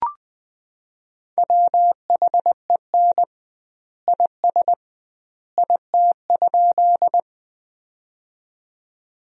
Morse code, heard as a keyed tone, about 20 wpm, 700 Hz. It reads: WHEN IS IT?
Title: 2x Word-Spacing - Sentences from Top 100 Words - 20wpm